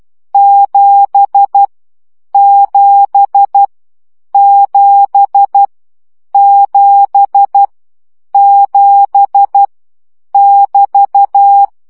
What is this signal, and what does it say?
Morse code, heard as a keyed tone, about 12 wpm, 800 Hz. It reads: 7 7 7 7 7 <BT>
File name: CW gyakorlas 7 10 baud